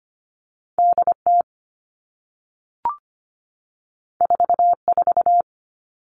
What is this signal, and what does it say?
Morse code, heard as a keyed tone, about 25 wpm, 700 Hz. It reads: DT 44
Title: All Cut Numbers - 2-Digit 25wpm